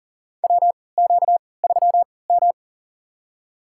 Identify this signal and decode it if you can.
Morse code, heard as a keyed tone, about 39 wpm, spaced as for 23 wpm, 700 Hz. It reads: WQ3M
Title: USA Call Signs - 4 Characters - 3x Character Spacing - 40wpm